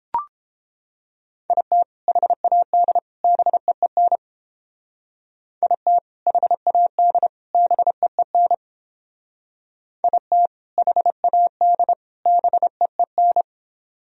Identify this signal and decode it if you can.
Morse code, heard as a keyed tone, about 30 wpm, 700 Hz. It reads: IT HAD BEEN IT HAD BEEN IT HAD BEEN